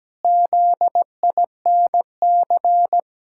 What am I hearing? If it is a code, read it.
Morse code, heard as a keyed tone, about 17 wpm, 700 Hz. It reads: ZINC
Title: Pangram Challenge v2 - 17wpm